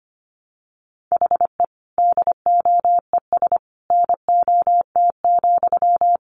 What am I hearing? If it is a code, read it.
Morse code, heard as a keyed tone, about 25 wpm, 700 Hz. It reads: HE DOES NOT,
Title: Sentences from Top 1200 Words 25wpm